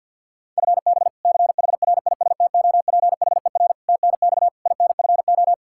Morse code, heard as a keyed tone, about 52 wpm, 700 Hz. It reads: FB CHRISTOPHER TNX INFO
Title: K1USN SST QSO Practice 50wpm